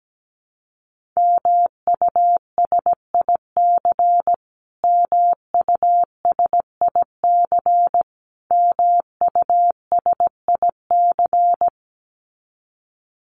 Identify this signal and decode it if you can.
Morse code, heard as a keyed tone, about 17 wpm, 700 Hz. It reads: MUSIC MUSIC MUSIC
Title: Top 300 Words Repeated 17wpm